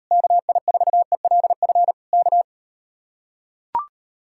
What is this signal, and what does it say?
Morse code, heard as a keyed tone, about 38 wpm, 700 Hz. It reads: KI4ELF K